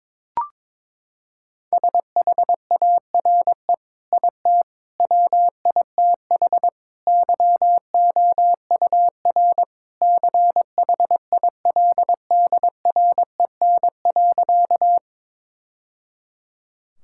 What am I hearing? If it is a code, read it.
Morse code, heard as a keyed tone, about 22 wpm, 700 Hz. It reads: SHARE IT WITH YOUR CHILDREN.
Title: Sentences from Top 1000 Words - Review Part 6 - 22wpm